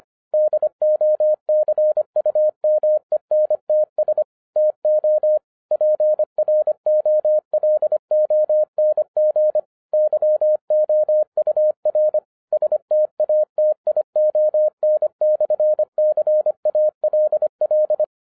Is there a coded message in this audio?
Yes: Morse code from 0.3 to 18.0 s, heard as a keyed tone, about 26 wpm, 610 Hz. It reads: DOCUMENTS TO PROLONG YOUR STATION/CALL